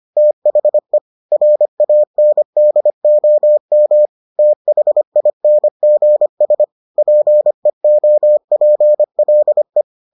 Morse code, heard as a keyed tone, about 25 wpm, 600 Hz. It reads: THE RANDOM THINGS PEOPLE